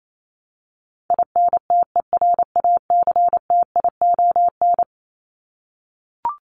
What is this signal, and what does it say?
Morse code, heard as a keyed tone, about 28 wpm, 700 Hz. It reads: INTERACTION
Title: Top 1251-1500 Words 28wpm